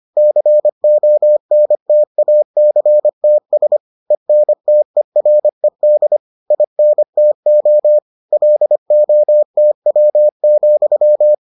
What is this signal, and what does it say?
Morse code, heard as a keyed tone, about 25 wpm, 600 Hz. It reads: CONTACTS ENTERED INTO LOTW,